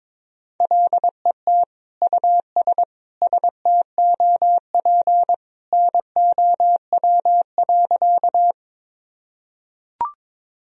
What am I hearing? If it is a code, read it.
Morse code, heard as a keyed tone, about 21 wpm, 700 Hz. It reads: LET US STOP NOW.
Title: Sentences from Top 1000 Words - Review Part 1 - 22wpm